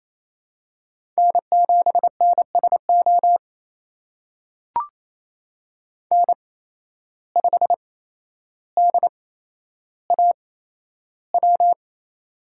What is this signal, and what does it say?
Morse code, heard as a keyed tone, about 28 wpm, spaced as for 12 wpm, 700 Hz. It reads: N7NSO N5DAW
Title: USA Call Signs - 5 Characters - 8x Character Spacing - 28wpm